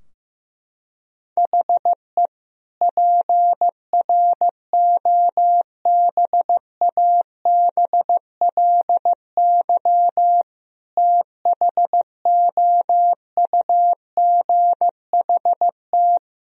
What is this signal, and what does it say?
Morse code, heard as a keyed tone, about 15 wpm, 700 Hz. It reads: HE PROBABLY THOUGHT
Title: Sentences from Top 1000 Words - Review Part 4 - 15wpm